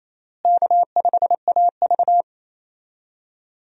Morse code, heard as a keyed tone, about 28 wpm, 700 Hz. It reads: K5AV
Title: USA Call Signs - 4 Characters - Repeated 3x - 28wpm